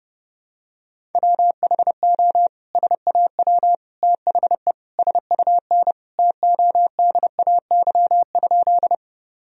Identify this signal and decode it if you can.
Morse code, heard as a keyed tone, about 30 wpm, 700 Hz. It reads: WHO SAW THE SUN TODAY?
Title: Sentences from Top 1000 Words - Review Part 1 - 30wpm